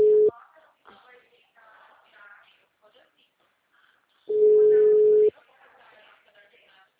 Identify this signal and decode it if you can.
Ringback tone (Europe, 425 Hz), with speech.